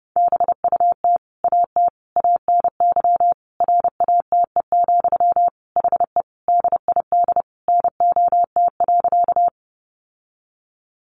Morse code, heard as a keyed tone, about 30 wpm, 700 Hz. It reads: BUT AT ANY RATE, HE DID NOT.
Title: Sentences from Top 700 Words 30wpm